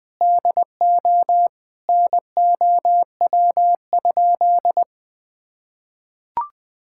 Morse code, heard as a keyed tone, about 20 wpm, 700 Hz. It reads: DO NOW?
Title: Sentences with 6-Word Phrases - Spoken Beginning - 20wpm